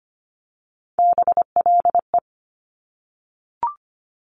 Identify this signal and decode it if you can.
Morse code, heard as a keyed tone, about 25 wpm, 700 Hz. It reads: BLE
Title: Top Three Letter Combinations in Words 25wpm